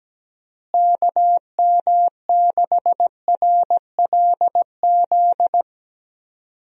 Morse code, heard as a keyed tone, about 17 wpm, 700 Hz. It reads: KM6RLZ
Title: USA Call Signs - 6 Characters - Repeated 3x - 17wpm